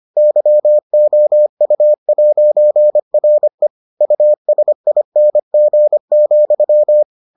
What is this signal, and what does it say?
Morse code, heard as a keyed tone, about 25 wpm, 600 Hz. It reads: YOU'RE USING,